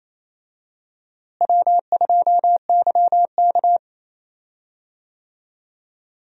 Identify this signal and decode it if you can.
Morse code, heard as a keyed tone, about 27 wpm, 700 Hz. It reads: W2YK
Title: USA Call Signs - 4 Characters - Repeated 2x - 28wpm